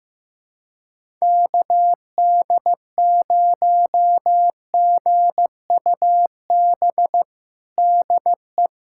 Morse code, heard as a keyed tone, about 15 wpm, 700 Hz. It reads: KD0GUB DE
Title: USA Callsign DE Callsign 15wpm